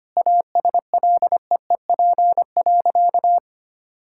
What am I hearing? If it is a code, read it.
Morse code, heard as a keyed tone, about 25 wpm, 700 Hz. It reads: ASLEEP.
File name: practice / 25wpm / Sentences with 3-Word Phrases - 25wpm